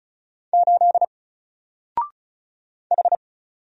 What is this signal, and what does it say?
Morse code, heard as a keyed tone, about 35 wpm, 700 Hz. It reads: 8 H
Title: Single Character (Letter:Number:Punctuation:Prosign) 35wpm